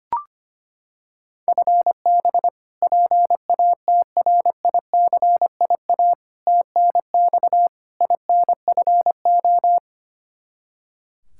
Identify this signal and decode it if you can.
Morse code, heard as a keyed tone, about 25 wpm, 700 Hz. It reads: FB PATRICIA TNX INFO
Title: Practice QSO - Exchange 3 - More Info - 25wpm